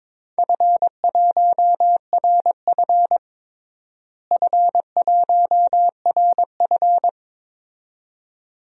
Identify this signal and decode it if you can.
Morse code, heard as a keyed tone, about 21 wpm, 700 Hz. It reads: F1RF F1RF
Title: DXCC-100 Call Signs - 4 Characters - Repeated 2x - 22wpm